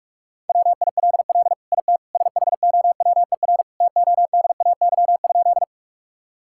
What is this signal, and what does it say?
Morse code, heard as a keyed tone, about 45 wpm, 700 Hz. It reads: WILL IT SHOWER TODAY?
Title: Sentences from Top 1600 Words 45wpm